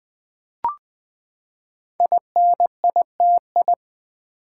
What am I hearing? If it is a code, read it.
Morse code, heard as a keyed tone, about 20 wpm, 700 Hz. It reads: INITI
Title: Build A Word - 5 Letters At A Time - 10-Letter Words - 20wpm